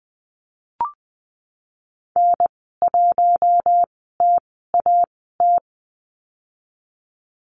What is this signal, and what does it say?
Morse code, heard as a keyed tone, about 20 wpm, spaced as for 15 wpm, 700 Hz. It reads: N1TAT
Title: 055 Lesson 10g - Letter r - Call Signs 20wpm